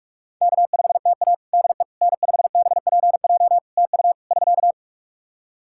Morse code, heard as a keyed tone, about 42 wpm, 700 Hz. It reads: K5TA DE N5BPJ TU <SK>
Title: K1USN SST QSO Practice 45wpm